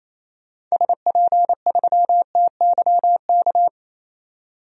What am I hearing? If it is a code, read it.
Morse code, heard as a keyed tone, about 28 wpm, 700 Hz. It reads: SP3TYK
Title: DXCC-100 Call Signs - 6 Characters - 28wpm